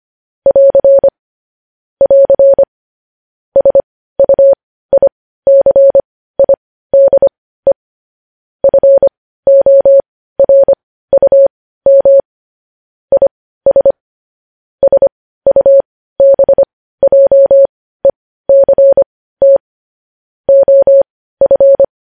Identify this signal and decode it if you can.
Morse code, heard as a keyed tone, about 25 wpm, spaced as for 15 wpm, 550 Hz. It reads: <AR> <AR> SUICIDE FORUM IS SUBJECT OF
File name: podcast / News headlines-1x-25x15